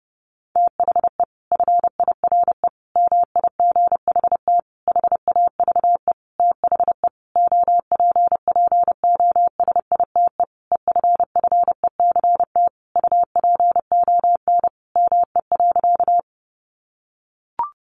Morse code, heard as a keyed tone, about 30 wpm, 700 Hz. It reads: THE FIRE MIGHT HAVE THE OPPOSITE EFFECT UPON ME.
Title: Sentences from Top 1300 Words 30wpm